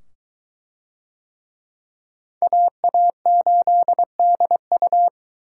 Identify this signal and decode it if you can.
Morse code, heard as a keyed tone, about 22 wpm, 700 Hz. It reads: AA8DU